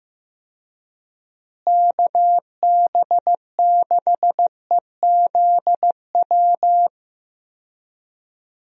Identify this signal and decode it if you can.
Morse code, heard as a keyed tone, about 15 wpm, 700 Hz. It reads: KB6EZW
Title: USA Call Signs - 6 Characters - Repeated 2x - 15wpm